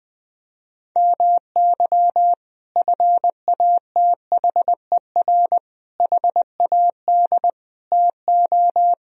Morse code, heard as a keyed tone, about 19 wpm, 700 Hz. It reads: MY FATHER HAD TO
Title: Sentences from Top 1000 Words - Review Part 1 - 20wpm